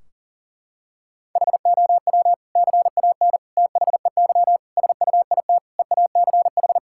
Morse code, heard as a keyed tone, about 40 wpm, 700 Hz. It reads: HOW CAN THEY SUIT EACH
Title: Sentences from Top 1600 Words 40wpm